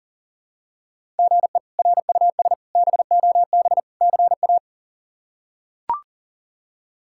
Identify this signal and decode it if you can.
Morse code, heard as a keyed tone, about 40 wpm, 700 Hz. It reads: GE RUS BOB CA
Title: K1USN SST QSO Practice - 40wpm